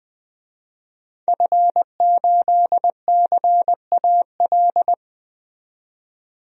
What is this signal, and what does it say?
Morse code, heard as a keyed tone, about 21 wpm, 700 Hz. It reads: F8CAL
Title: DXCC-100 Call Signs - 5 Characters - 20wpm